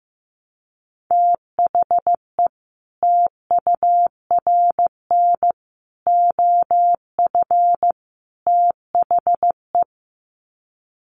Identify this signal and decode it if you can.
Morse code, heard as a keyed tone, about 15 wpm, 700 Hz. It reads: THE TURN OF THE
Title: Sets of 4 Words 15wpm